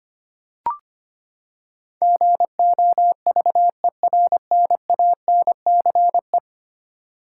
Morse code, heard as a keyed tone, about 25 wpm, 700 Hz. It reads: GOVERNANCE
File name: Ten Letter Words - 25wpm